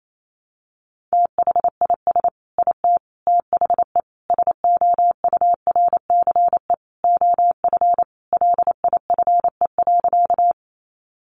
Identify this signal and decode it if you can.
Morse code, heard as a keyed tone, about 28 wpm, 700 Hz. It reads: THIS IT THE SOURCE OF LIFE.
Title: Sentences from Top 800 Words 28wpm